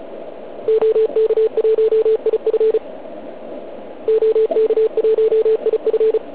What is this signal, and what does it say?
Morse code, heard as a keyed tone, about 36 wpm, 440 Hz. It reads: OK1IF OK1IF